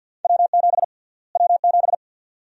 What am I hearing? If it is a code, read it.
Morse code, heard as a keyed tone, about 51 wpm, 700 Hz. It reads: W7 W7
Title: USA Call Sign Prefixes - Repeated 2x - 50wpm